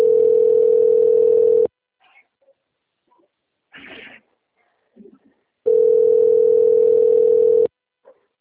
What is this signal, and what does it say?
Ringback tone (North America), 440 + 480 Hz.